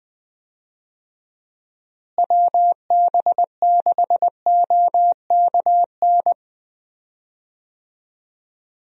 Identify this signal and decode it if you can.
Morse code, heard as a keyed tone, about 20 wpm, 700 Hz. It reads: WB6OKN
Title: USA Call Signs - 6 Characters - Repeated 3x - 20wpm